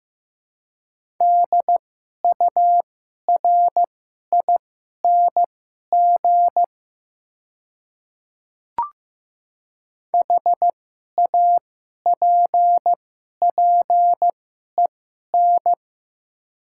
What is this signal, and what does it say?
Morse code, heard as a keyed tone, about 15 wpm, spaced as for 11 wpm, 700 Hz. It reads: DURING HAPPEN
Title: 2x Character Spacing - Top 500 Words - 15wpm